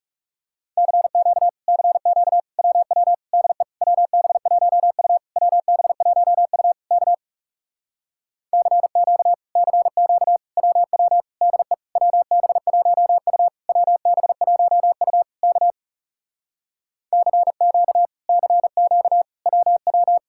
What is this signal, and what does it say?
Morse code, heard as a keyed tone, about 40 wpm, 700 Hz. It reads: CQ CQ WW DE WB1U WB1U K CQ CQ WW DE WB1U WB1U K CQ CQ WW